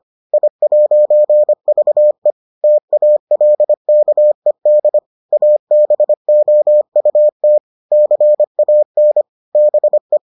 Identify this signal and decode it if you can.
Morse code, heard as a keyed tone, about 25 wpm, 600 Hz. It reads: I'VE TALKED ABOUT CAN BE